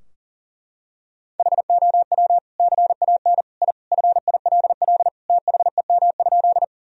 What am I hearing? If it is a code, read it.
Morse code, heard as a keyed tone, about 37 wpm, 700 Hz. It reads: HOW CAN I FILL THEM?
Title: Sentences from Top 1000 Words - Review Part 3 - 40wpm